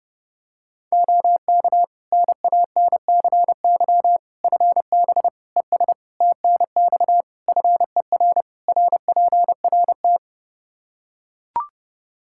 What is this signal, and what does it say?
Morse code, heard as a keyed tone, about 29 wpm, 700 Hz. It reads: OK NANCY FB ES TNX FER RPRT
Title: Practice QSO - Exchange 2 - Info - 30wpm